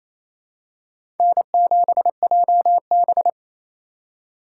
Morse code, heard as a keyed tone, about 28 wpm, 700 Hz. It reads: N7JB
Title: USA Call Signs - 4 Characters - 2x Character Spacing - 28wpm